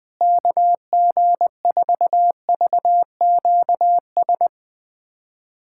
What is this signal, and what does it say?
Morse code, heard as a keyed tone, about 20 wpm, 700 Hz. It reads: KG4VQS
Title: USA Callsign DE Callsign 20wpm